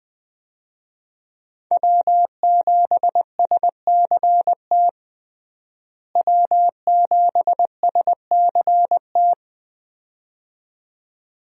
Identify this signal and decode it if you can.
Morse code, heard as a keyed tone, about 20 wpm, 700 Hz. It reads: W7SCT W7SCT
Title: USA Call Signs - 5 Characters - Repeated 2x - 20wpm